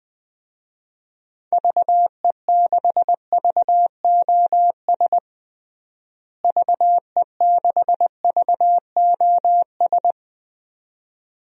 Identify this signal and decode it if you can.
Morse code, heard as a keyed tone, about 20 wpm, 700 Hz. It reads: VE6VOS VE6VOS
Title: DXCC-100 Call Signs - 6 Characters - Repeated 2x - 20wpm